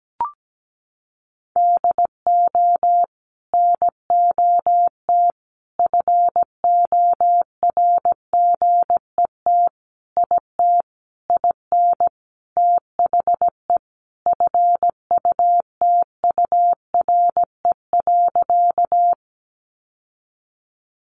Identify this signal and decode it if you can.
Morse code, heard as a keyed tone, about 17 wpm, 700 Hz. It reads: DO NOT FORGET IT IN THE FUTURE.
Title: Sentences from Top 1200 Words 17wpm